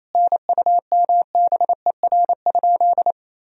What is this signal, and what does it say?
Morse code, heard as a keyed tone, about 28 wpm, 700 Hz. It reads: NUMBER?
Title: Sentences from Top 1000 Words - Review Part 1 - 28wpm